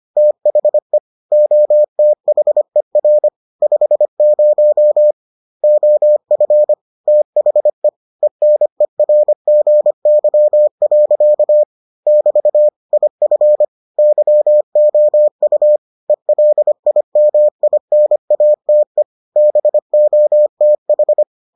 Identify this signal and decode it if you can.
Morse code, heard as a keyed tone, about 25 wpm, 600 Hz. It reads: THE OTHER 50 OF THE ENERGY. <BT> IF YOU ELIMINATE BOTH